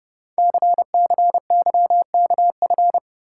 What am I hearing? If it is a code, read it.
Morse code, heard as a keyed tone, about 30 wpm, 700 Hz. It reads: CCYKF
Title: Code Groups - 5 Letters - 30wpm